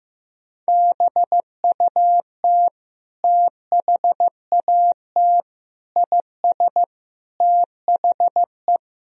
Morse code, heard as a keyed tone, about 16 wpm, 700 Hz. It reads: BUT THAT IS THE
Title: Sentences from Top 1000 Words - Review Part 3 - 15wpm